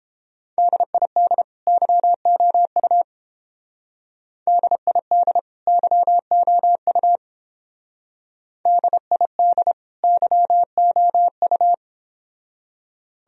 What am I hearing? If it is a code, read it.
Morse code, heard as a keyed tone, about 29 wpm, 700 Hz. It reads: DID YOU DID YOU DID YOU